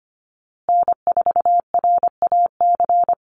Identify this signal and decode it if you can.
Morse code, heard as a keyed tone, about 25 wpm, 700 Hz. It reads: N4RAC
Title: USA Call Signs Repeated 2x 25wpm